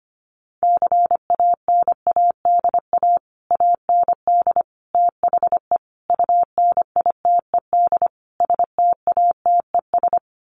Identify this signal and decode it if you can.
Morse code, heard as a keyed tone, about 25 wpm, 700 Hz. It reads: CANADA AND THE UNITED STATES
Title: Sets of 5 Words Encore 25wpm